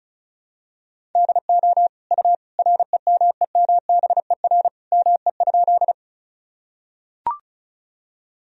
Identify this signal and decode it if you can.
Morse code, heard as a keyed tone, about 34 wpm, 700 Hz. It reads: DO U REMEMBER ME?
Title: Sentences from Top 400 Words and Abbreviations 35wpm